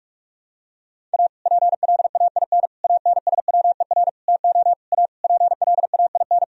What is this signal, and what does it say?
Morse code, heard as a keyed tone, about 42 wpm, 700 Hz. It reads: A PLAIN ANSWER TO A PLAIN
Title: Sentences from Top 1000 Words - Review Part 2 - 45wpm